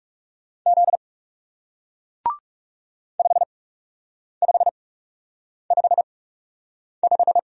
Morse code, heard as a keyed tone, about 41 wpm, spaced as for 8 wpm, 700 Hz. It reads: Z 5555